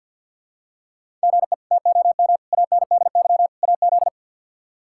Morse code, heard as a keyed tone, about 46 wpm, 700 Hz. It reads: GE TOM ANDY AZ
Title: K1USN SST QSO Practice - 50wpm